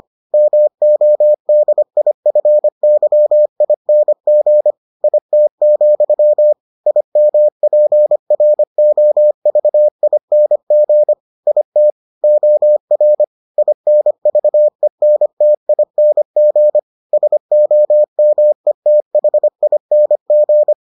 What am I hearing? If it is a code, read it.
Morse code, heard as a keyed tone, about 25 wpm, 600 Hz. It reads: MODIFYING IT, IMPROVING IT OR INVENTING SOMETHING